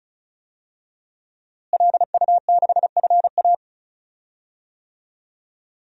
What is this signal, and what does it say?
Morse code, heard as a keyed tone, about 36 wpm, 700 Hz. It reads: LU6FA
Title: DXCC-100 Call Signs - 5 Characters - Repeated 2x - 35wpm